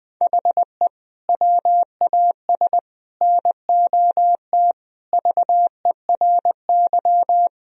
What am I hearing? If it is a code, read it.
Morse code, heard as a keyed tone, about 20 wpm, 700 Hz. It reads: HE WAS NOT VERY
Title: Sentences from Top 1000 Words - Review Part 3 - 20wpm